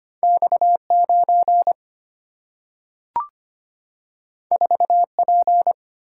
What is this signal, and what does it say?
Morse code, heard as a keyed tone, about 25 wpm, 700 Hz. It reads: X9 4P
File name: Two Letter-Number Combinations - 25wpm